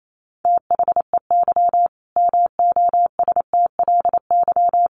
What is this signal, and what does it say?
Morse code, heard as a keyed tone, about 28 wpm, 700 Hz. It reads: THEY MOSTLY
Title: Sentences from Top 1300 Words 28wpm